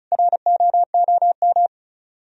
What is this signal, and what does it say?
Morse code, heard as a keyed tone, about 44 wpm, 700 Hz. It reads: ROOM